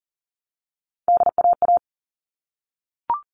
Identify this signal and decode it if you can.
Morse code, heard as a keyed tone, about 40 wpm, 700 Hz. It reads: DAA
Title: All Cut Numbers - 3-Digit 40wpm